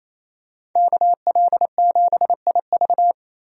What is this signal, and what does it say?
Morse code, heard as a keyed tone, about 28 wpm, 700 Hz. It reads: KL7IV
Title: DXCC-100 Call Signs - 5 Characters - 28wpm